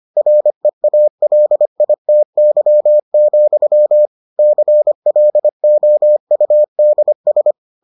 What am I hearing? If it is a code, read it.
Morse code, heard as a keyed tone, about 25 wpm, 600 Hz. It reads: REALITY, CLOUDS